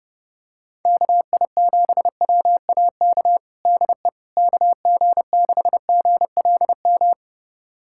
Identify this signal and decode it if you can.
Morse code, heard as a keyed tone, about 32 wpm, 700 Hz. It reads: KI7WAK DE KG6GLM
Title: USA Callsign DE Callsign - 6 Characters - 30wpm